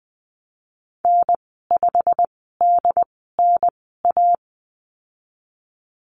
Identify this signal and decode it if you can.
Morse code, heard as a keyed tone, about 20 wpm, spaced as for 15 wpm, 700 Hz. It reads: N5DNA
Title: 193 Lesson 31g - Letter v - Call Signs 20wpm